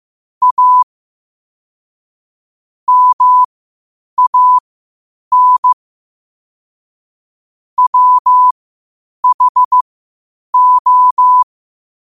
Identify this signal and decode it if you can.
Morse code, heard as a keyed tone, about 15 wpm, spaced as for 8 wpm, 1000 Hz. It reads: A MAN WHO
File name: qotdfiles / 8wpm